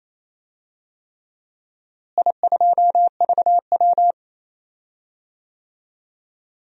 Morse code, heard as a keyed tone, about 28 wpm, 700 Hz. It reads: I2VW
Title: DXCC-100 Call Signs - 4 Characters - Repeated 3x - 28wpm